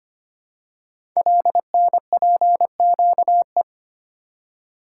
Morse code, heard as a keyed tone, about 25 wpm, 700 Hz. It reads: LNPQE
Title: Code Groups - 5 Letters - 25wpm